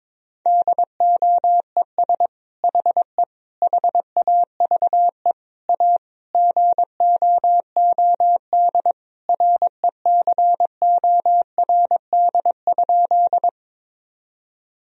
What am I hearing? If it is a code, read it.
Morse code, heard as a keyed tone, about 22 wpm, 700 Hz. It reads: DOES HE HAVE A GOOD RECORD?
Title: Sentences from Top 1000 Words - Review Part 3 - 22wpm